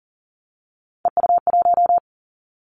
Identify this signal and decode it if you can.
Morse code, heard as a keyed tone, about 43 wpm, 700 Hz. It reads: EU1
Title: Callsign Prefixes for Top 50 DXCC - 40wpm